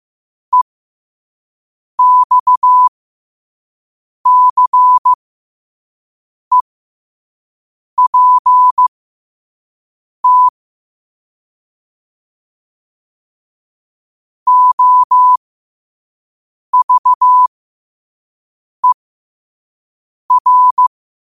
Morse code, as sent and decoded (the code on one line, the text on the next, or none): . -..- -.-. .--. - / --- ...- . .-.
EXCEPT OVER